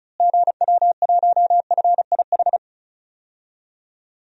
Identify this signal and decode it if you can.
Morse code, heard as a keyed tone, about 36 wpm, 700 Hz. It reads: GW1FIH
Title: DXCC-100 Call Signs - 6 Characters - Repeated 3x - 35wpm